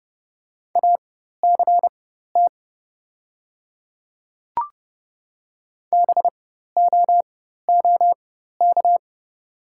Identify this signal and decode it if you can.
Morse code, heard as a keyed tone, about 30 wpm, spaced as for 14 wpm, 700 Hz. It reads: ACT BOOK
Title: 4x Character Spacing - Top 300 Words - 30wpm